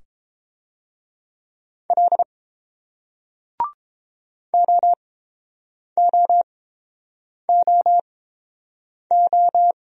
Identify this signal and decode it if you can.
Morse code, heard as a keyed tone, about 30 wpm, spaced as for 7 wpm, 700 Hz. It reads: L OOOO